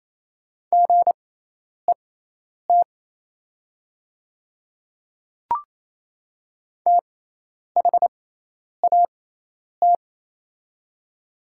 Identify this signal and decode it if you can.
Morse code, heard as a keyed tone, about 27 wpm, spaced as for 10 wpm, 700 Hz. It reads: GET THAT
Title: 6x Character Spacing - Top 300 Words - 28wpm